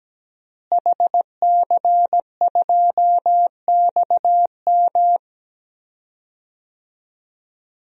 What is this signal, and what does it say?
Morse code, heard as a keyed tone, about 17 wpm, 700 Hz. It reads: HC2XM
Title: DXCC-100 Call Signs - 5 Characters - Repeated 2x - 17wpm